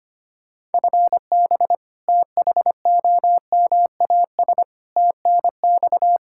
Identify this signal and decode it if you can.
Morse code, heard as a keyed tone, about 25 wpm, 700 Hz. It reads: FB THOMAS TNX
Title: Practice QSO - Exchange 3 - More Info - 25wpm